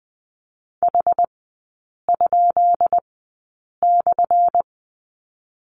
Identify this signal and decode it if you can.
Morse code, heard as a keyed tone, about 20 wpm, spaced as for 8 wpm, 700 Hz. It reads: H?/
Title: 178 Lesson 29e - Slash - Three Characters 20wpm